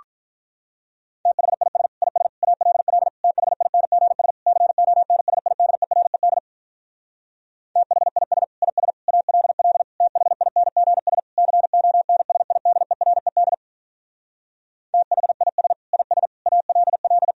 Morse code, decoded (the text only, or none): THIS IS ALL THINGS CONSIDERED THIS IS ALL THINGS CONSIDERED THIS IS ALL